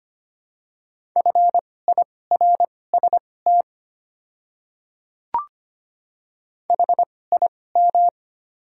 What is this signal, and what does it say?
Morse code, heard as a keyed tone, about 25 wpm, spaced as for 18 wpm, 690 Hz. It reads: FIRST HIM
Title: 2x Character Spacing - Top 100 Words - 25wpm